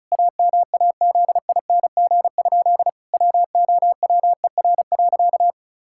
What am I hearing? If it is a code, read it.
Morse code, heard as a keyed tone, about 36 wpm, 700 Hz. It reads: AMAZING? WOWER.
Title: Pangram Challenge v2 - 35wpm